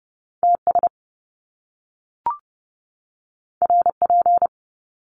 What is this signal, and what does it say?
Morse code, heard as a keyed tone, about 30 wpm, 700 Hz. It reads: TS RP
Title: Top Two Letter Combinations in Words 30wpm